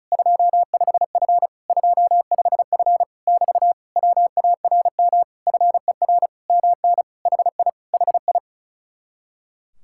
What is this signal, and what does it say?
Morse code, heard as a keyed tone, about 33 wpm, 700 Hz. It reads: 25F 25F <BT> WARM FER MN HI HI
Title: Speed-Racing Common QSO Phrases Encore - 35wpm